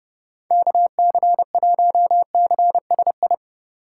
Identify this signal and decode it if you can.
Morse code, heard as a keyed tone, about 29 wpm, 700 Hz. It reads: KC1CSI